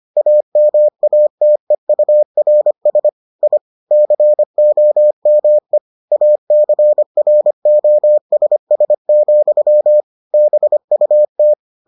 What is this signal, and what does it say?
Morse code, heard as a keyed tone, about 25 wpm, 600 Hz. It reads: AMATEURS I COME ACROSS, BUT